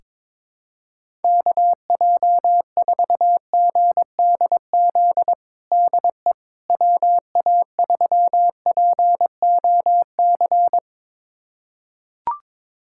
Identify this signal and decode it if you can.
Morse code, heard as a keyed tone, about 22 wpm, 700 Hz. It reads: KJ4GDZ DE WA3POC
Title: USA Callsign DE Callsign 22wpm